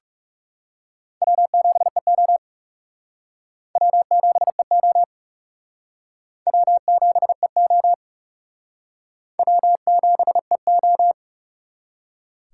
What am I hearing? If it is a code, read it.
Morse code, heard as a keyed tone, about 40 wpm, 700 Hz. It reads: W7EO W7EO W7EO W7EO